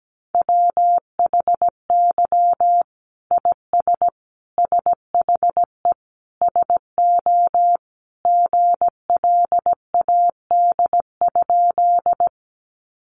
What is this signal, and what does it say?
Morse code, heard as a keyed tone, about 17 wpm, 700 Hz. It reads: WHY IS SHE SO GLAD?
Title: Sentences from Top 800 Words 17wpm